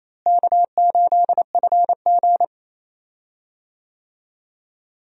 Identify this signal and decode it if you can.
Morse code, heard as a keyed tone, about 27 wpm, 700 Hz. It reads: K8FG
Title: USA Call Signs - 4 Characters - Repeated 3x - 28wpm